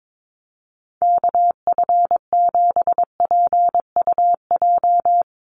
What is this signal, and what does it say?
Morse code, heard as a keyed tone, about 22 wpm, 700 Hz. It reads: KF7PUJ
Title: USA Call Signs 22wpm